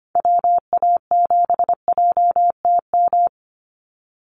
Morse code, heard as a keyed tone, about 25 wpm, 700 Hz. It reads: WA7JTM
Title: NA SOTA Chaser Callsigns 25wpm